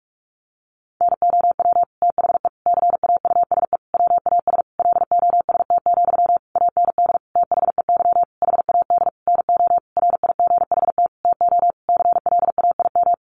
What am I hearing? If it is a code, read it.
Morse code, heard as a keyed tone, about 44 wpm, 700 Hz. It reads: NOW THE CAUSE WAS LOST, AND THEY HAD NO RIGHT TO CLAIM
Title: Sentences from Top 1500 Words 45wpm